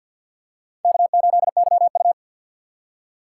Morse code, heard as a keyed tone, about 46 wpm, 700 Hz. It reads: K8YU
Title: USA Call Signs - 4 Characters - 3x Character Spacing - 50wpm